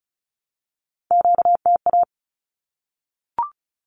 Morse code, heard as a keyed tone, about 34 wpm, 700 Hz. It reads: QTA
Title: Q-Codes Repeated 3x 35wpm